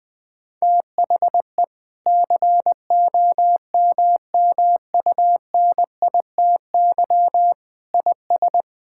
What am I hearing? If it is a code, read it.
Morse code, heard as a keyed tone, about 21 wpm, 700 Hz. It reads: THE COMMUNITY IS
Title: Sentences from Top 1000 Words - Review Part 6 - 20wpm